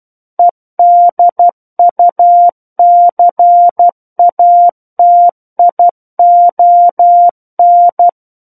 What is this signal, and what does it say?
Morse code, heard as a keyed tone, about 12 wpm, 700 Hz. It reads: EDUCATION